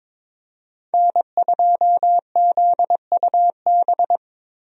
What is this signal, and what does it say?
Morse code, heard as a keyed tone, about 21 wpm, 700 Hz. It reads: N2ZUB
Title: USA Call Signs - 5 Characters - Repeated 3x - 22wpm